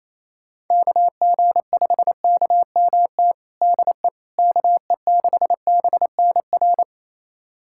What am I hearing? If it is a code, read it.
Morse code, heard as a keyed tone, about 28 wpm, 700 Hz. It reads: KG5KMT DE KE6BNR
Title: USA Callsign DE Callsign - 6 Characters - 28wpm